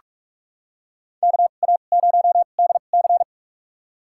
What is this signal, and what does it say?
Morse code, heard as a keyed tone, about 46 wpm, spaced as for 33 wpm, 700 Hz. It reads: KA0DC
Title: USA Call Signs - 5 Characters - 2x Character Spacing - 45wpm